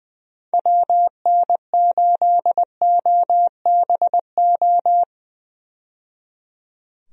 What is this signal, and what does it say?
Morse code, heard as a keyed tone, about 20 wpm, 700 Hz. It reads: WN8OBO
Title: USA Call Signs - 6 Characters - 20wpm